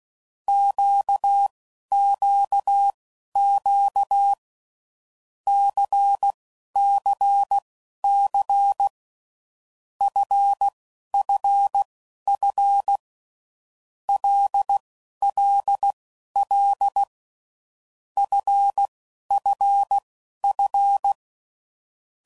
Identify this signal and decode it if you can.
Morse code, heard as a keyed tone, about 16 wpm, spaced as for 11 wpm, 800 Hz. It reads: QQQ CCC FFF LLL FFF
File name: Groups / Basic Letter Groups / LFCYQ